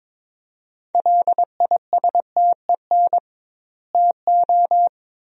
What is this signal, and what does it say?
Morse code, heard as a keyed tone, about 22 wpm, 700 Hz. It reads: LISTEN TO
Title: 2x Word-Spacing - Sentences from Top 400 Words - 22wpm